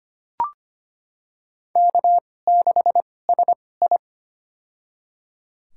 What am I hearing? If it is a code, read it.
Morse code, heard as a keyed tone, about 25 wpm, spaced as for 18 wpm, 700 Hz. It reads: K6SI
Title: USA Call Signs - 2x Character Spacing - 25wpm